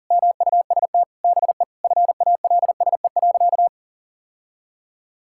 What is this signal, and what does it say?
Morse code, heard as a keyed tone, about 37 wpm, 700 Hz. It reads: MUST BE FALSE.
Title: Sentences from Top 1600 Words 40wpm